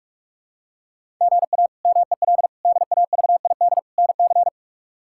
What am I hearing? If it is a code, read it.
Morse code, heard as a keyed tone, about 46 wpm, 700 Hz. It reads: GA MEL DAVID NC
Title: K1USN SST QSO Practice - 45wpm